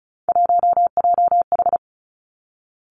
Morse code, heard as a keyed tone, about 36 wpm, 700 Hz. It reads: 1JH
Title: Three Challenging Letter-Number Combinations 35wpm